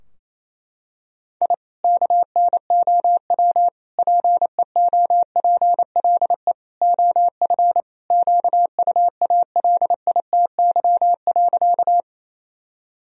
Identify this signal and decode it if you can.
Morse code, heard as a keyed tone, about 27 wpm, 700 Hz. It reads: I KNOW PEOPLE OF QUALITY.
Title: Sentences from Top 1000 Words - Review Part 4 - 28wpm